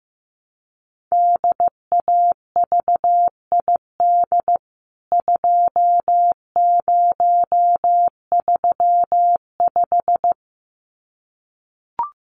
Morse code, heard as a keyed tone, about 15 wpm, 700 Hz. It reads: DAVID 2035
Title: CWT Exchange 15wpm